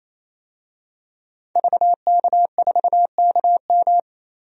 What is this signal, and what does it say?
Morse code, heard as a keyed tone, about 28 wpm, 700 Hz. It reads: VK4KM
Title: DXCC-100 Call Signs - 5 Characters - 28wpm